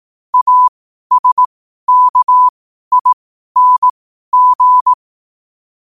Morse code, heard as a keyed tone, about 18 wpm, spaced as for 12 wpm, 1000 Hz. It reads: ASKING
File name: qotdfiles / 12wpm